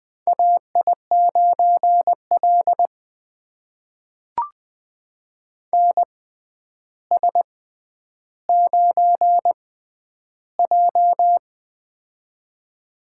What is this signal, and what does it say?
Morse code, heard as a keyed tone, about 20 wpm, spaced as for 10 wpm, 700 Hz. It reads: AI9L NS9J
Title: USA Call Signs - 4 Characters - 6x Character Spacing - 20wpm